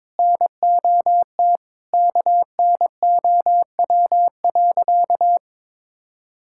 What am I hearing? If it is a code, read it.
Morse code, heard as a keyed tone, about 22 wpm, 700 Hz. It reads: NOT KNOW.
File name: practice / 22wpm / Sentences with 4-Word Phrases - 22wpm